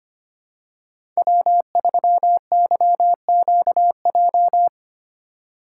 Morse code, heard as a keyed tone, about 25 wpm, 700 Hz. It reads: W3YQJ
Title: Code Groups - 5 Letter:Numbers - 25wpm